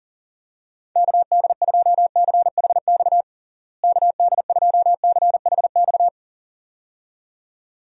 Morse code, heard as a keyed tone, about 40 wpm, 700 Hz. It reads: KD2CHX KD2CHX
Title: USA Call Signs - 6 Characters - Repeated 2x - 40wpm